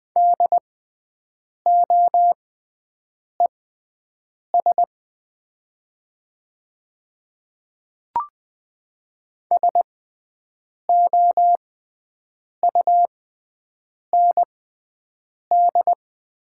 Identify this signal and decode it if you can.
Morse code, heard as a keyed tone, about 20 wpm, spaced as for 7 wpm, 700 Hz. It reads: DOES SOUND
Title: 6x Character Spacing - Top 200 Words - 20wpm